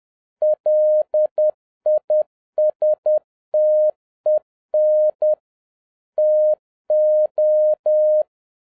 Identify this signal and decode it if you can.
Morse code, heard as a keyed tone, about 10 wpm, 610 Hz. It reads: LISTEN TO